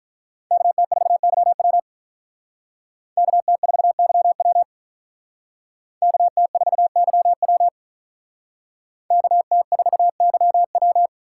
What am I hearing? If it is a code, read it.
Morse code, heard as a keyed tone, about 42 wpm, 700 Hz. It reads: KT4YW KT4YW KT4YW KT4YW
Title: Speed Racing - USA 2x2 Call Signs 35wpm